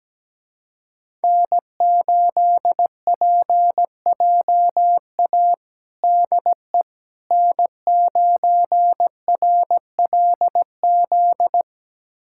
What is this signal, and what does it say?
Morse code, heard as a keyed tone, about 17 wpm, 700 Hz. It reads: N8PJA DE N9RLZ
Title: USA Callsign DE Callsign - 5 Characters - 17wpm